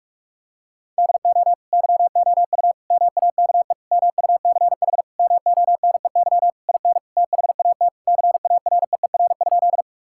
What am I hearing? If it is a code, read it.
Morse code, heard as a keyed tone, about 46 wpm, 700 Hz. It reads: DO YOU MAKE MUCH MONEY IN THAT CAREER?
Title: Sentences from Top 1000 Words - Review Part 6 - 45wpm